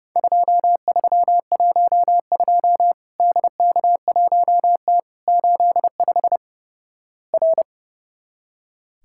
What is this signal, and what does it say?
Morse code, heard as a keyed tone, about 29 wpm, 700 Hz. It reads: 2312 DK1T 85 R
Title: Worked All Europe DX Contest - QTC Exchange - 30wpm